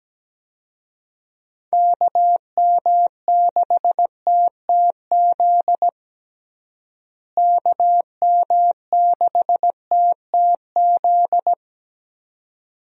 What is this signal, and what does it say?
Morse code, heard as a keyed tone, about 17 wpm, 700 Hz. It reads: KM6TTZ KM6TTZ
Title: USA Call Signs - 6 Characters - Repeated 2x - 17wpm